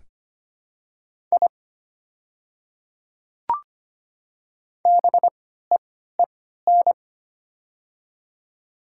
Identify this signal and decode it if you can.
Morse code, heard as a keyed tone, about 25 wpm, spaced as for 14 wpm, 700 Hz. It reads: I BEEN
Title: 3x Character Spacing - Top 100 Words - 25wpm